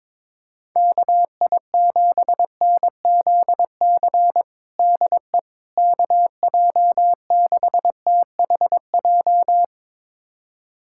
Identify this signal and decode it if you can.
Morse code, heard as a keyed tone, about 22 wpm, 700 Hz. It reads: KI7NZC DE KJ6THJ
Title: USA Callsign DE Callsign - 6 Characters - 22wpm